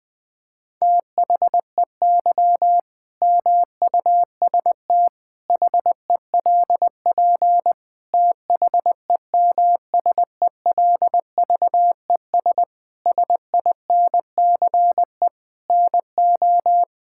Morse code, heard as a keyed tone, about 20 wpm, 700 Hz. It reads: THEY MUST HELP THEMSELVES SINCE NO
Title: Sentences from Top 1000 Words - Review Part 5 - 20wpm